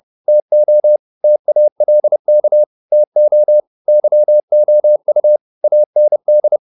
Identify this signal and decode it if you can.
Morse code, heard as a keyed tone, about 30 wpm, 600 Hz. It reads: TO TALK TO YOU AND
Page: advanced-practice-files